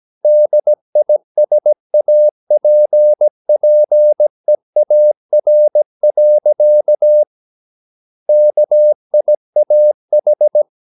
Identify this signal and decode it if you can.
Morse code, heard as a keyed tone, about 17 wpm, 600 Hz. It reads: DISAPPEAR. KIAH
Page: Romanen Never af Ken Follet på engelsk.